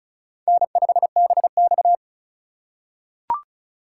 Morse code, heard as a keyed tone, about 34 wpm, 700 Hz. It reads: N5BX
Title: USA Call Signs - 4x Character Spacing - 35wpm